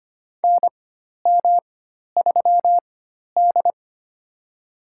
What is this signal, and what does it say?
Morse code, heard as a keyed tone, about 25 wpm, spaced as for 12 wpm, 700 Hz. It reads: NM3D
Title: USA Call Signs - 4 Characters - 4x Character Spacing - 25wpm